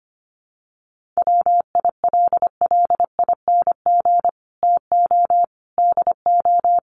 Morse code, heard as a keyed tone, about 25 wpm, 700 Hz. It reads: WILLING TO DO
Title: Sentences from Top 1400 Words 25wpm